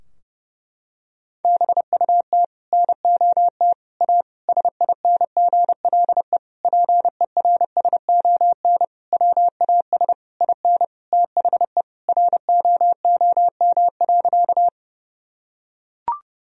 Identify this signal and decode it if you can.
Morse code, heard as a keyed tone, about 30 wpm, 700 Hz. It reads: BUT NOT A SINGLE PERSON WAS IN THE ROOM.
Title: Sentences from Top 1000 Words - Review Part 4 - 30wpm